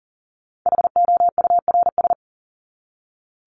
Morse code, heard as a keyed tone, about 40 wpm, 700 Hz. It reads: HOURS
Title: ES:S Suffix - Top 100 Words with Base Words 40wpm